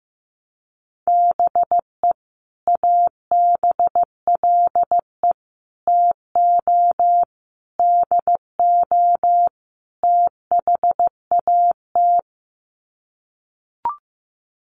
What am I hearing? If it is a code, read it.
Morse code, heard as a keyed tone, about 15 wpm, 700 Hz. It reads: BE ABLE TO DO THAT
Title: Sets of 5 Words Encore2 15wpm